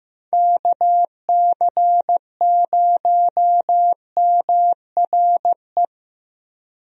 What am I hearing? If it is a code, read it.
Morse code, heard as a keyed tone, about 15 wpm, 700 Hz. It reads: KC0MRE
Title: USA Call Signs - 6 Characters - Repeated 2x - 15wpm